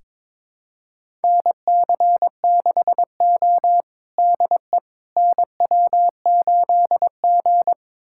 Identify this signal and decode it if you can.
Morse code, heard as a keyed tone, about 21 wpm, 700 Hz. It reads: NC6O DE NW8G
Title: USA Callsign DE Callsign - 4 Characters - 22wpm